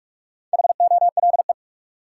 Morse code, heard as a keyed tone, about 46 wpm, 700 Hz. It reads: HOLE
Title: Sentences from Top 1600 Words - New Word Drill - 45wpm